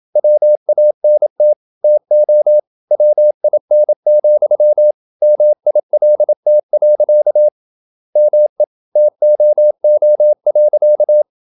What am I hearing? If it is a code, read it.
Morse code, heard as a keyed tone, about 27 wpm, 600 Hz. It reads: WANT TO WIN, MILT. ME TOO.
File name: Never 0027wpm